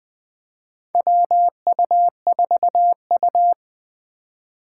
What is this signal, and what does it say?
Morse code, heard as a keyed tone, about 20 wpm, 700 Hz. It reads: WU4U
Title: USA Call Signs - 4 Characters - 4x Character Spacing - 20wpm